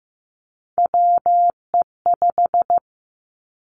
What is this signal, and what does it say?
Morse code, heard as a keyed tone, about 15 wpm, 700 Hz. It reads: WE5
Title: Three Challenging Letter-Number Combinations 15wpm